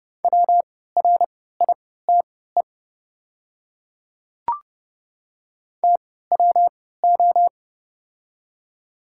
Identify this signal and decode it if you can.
Morse code, heard as a keyed tone, about 30 wpm, spaced as for 17 wpm, 700 Hz. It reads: WRITE TWO
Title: 3x Character Spacing - Top 100 Words - 30wpm